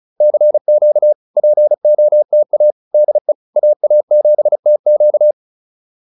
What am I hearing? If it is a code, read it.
Morse code, heard as a keyed tone, about 34 wpm, 590 Hz. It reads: CQ POTA DE AA7TQ